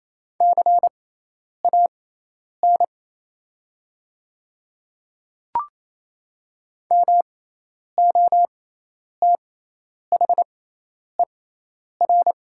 Morse code, heard as a keyed tone, about 28 wpm, spaced as for 10 wpm, 690 Hz. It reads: CAN MOTHER
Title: 6x Character Spacing - Top 200 Words - 28wpm